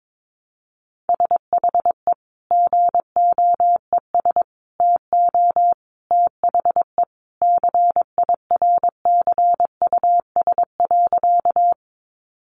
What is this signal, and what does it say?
Morse code, heard as a keyed tone, about 22 wpm, 700 Hz. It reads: SHE GOES TO THE CIRCUS.
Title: Easy Phrases 22wpm